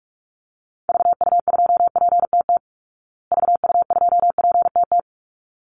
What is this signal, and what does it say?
Morse code, heard as a keyed tone, about 44 wpm, 700 Hz. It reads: VU2PTT VU2PTT
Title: Call Signs Repeated 3x - 45wpm